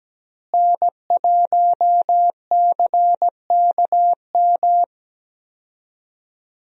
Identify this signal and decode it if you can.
Morse code, heard as a keyed tone, about 17 wpm, 700 Hz. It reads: N1CKM
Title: USA Call Signs - 5 Characters - Repeated 3x - 17wpm